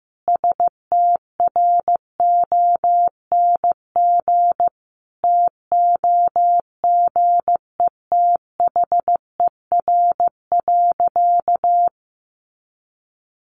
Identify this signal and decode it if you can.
Morse code, heard as a keyed tone, about 15 wpm, 700 Hz. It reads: STRONG TOGETHER.
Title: Sentences from Top 500 Words 15wpm